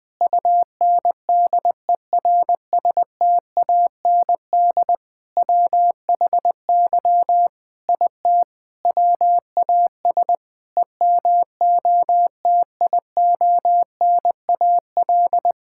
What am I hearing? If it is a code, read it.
Morse code, heard as a keyed tone, about 20 wpm, 700 Hz. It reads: UNDERSTAND WHY IT WAS EMOTIONAL
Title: Sentences from Top 1000 Words - Review Part 5 - 20wpm